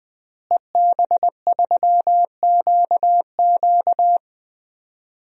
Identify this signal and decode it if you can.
Morse code, heard as a keyed tone, about 20 wpm, 700 Hz. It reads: EB3QQ
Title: DXCC-100 Call Signs - 5 Characters - Repeated 2x - 20wpm